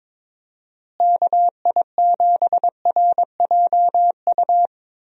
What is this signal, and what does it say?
Morse code, heard as a keyed tone, about 21 wpm, 700 Hz. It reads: KI7RJU
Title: USA Call Signs - 6 Characters - Repeated 3x - 22wpm